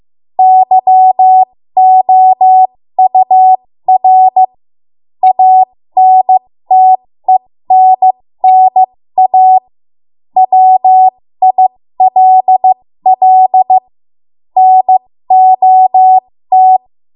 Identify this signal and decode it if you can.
Morse code, heard as a keyed tone, about 15 wpm, spaced as for 13 wpm, 750 Hz. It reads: YOUR ANTENNA WILL NOT
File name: Morse / 13WPM